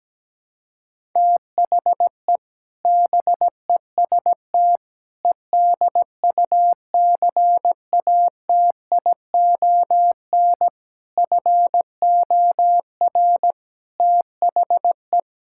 Morse code, heard as a keyed tone, about 17 wpm, 700 Hz. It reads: THE BEST EDUCATION FOR THE
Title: Sentences from Top 1000 Words - Review Part 6 - 17wpm